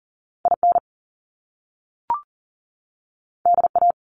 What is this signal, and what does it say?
Morse code, heard as a keyed tone, about 40 wpm, 700 Hz. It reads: IN DA
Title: Top Two Letter Combinations in Words 40wpm